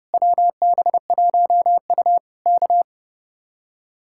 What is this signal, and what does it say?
Morse code, heard as a keyed tone, about 30 wpm, 700 Hz. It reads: WB1U K
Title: Speed-Racing Common QSO Phrases - 30wpm